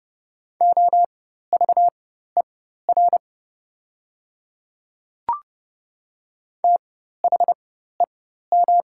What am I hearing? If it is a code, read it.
Morse code, heard as a keyed tone, about 29 wpm, spaced as for 14 wpm, 690 Hz. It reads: OVER THEM
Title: 4x Character Spacing - Top 100 Words - 30wpm